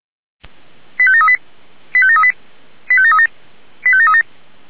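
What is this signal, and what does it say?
Five-tone selective call (ZVEI), heard as quick runs of five tones, tones of about 70 ms. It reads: 85738 85738 85737 85737
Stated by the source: ZVEI five-tone selective call received by SDR.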